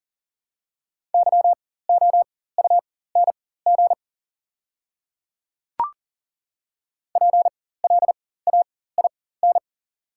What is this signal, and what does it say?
Morse code, heard as a keyed tone, about 40 wpm, spaced as for 19 wpm, 700 Hz. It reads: YOUNG PLAIN
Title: 4x Character Spacing - Top 500 Words - 40wpm